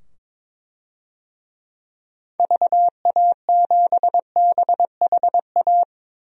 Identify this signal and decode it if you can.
Morse code, heard as a keyed tone, about 21 wpm, 700 Hz. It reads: VA7BHA
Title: DXCC-100 Call Signs - 6 Characters - 22wpm